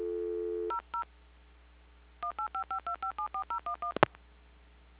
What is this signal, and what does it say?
DTMF, heard as telephone touch tones, tones of about 100 ms. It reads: 00 18552574011
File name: recordings / PortlandUnion Millennium